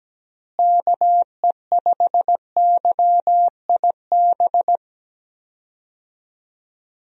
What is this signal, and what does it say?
Morse code, heard as a keyed tone, about 17 wpm, 700 Hz. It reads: KE5YIB
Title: USA Call Signs - 6 Characters - Repeated 3x - 17wpm